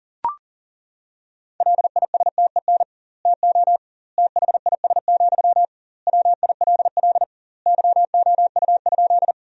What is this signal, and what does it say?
Morse code, heard as a keyed tone, about 40 wpm, 700 Hz. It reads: LISTEN TO THIS, WILL YOU?
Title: 2x Word-Spacing - Sentences from Top 400 Words - 40wpm